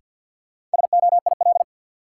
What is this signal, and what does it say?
Morse code, heard as a keyed tone, about 46 wpm, 700 Hz. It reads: SOIL
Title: Sentences from Top 1500 Words - New Word Drill Alt - 50wpm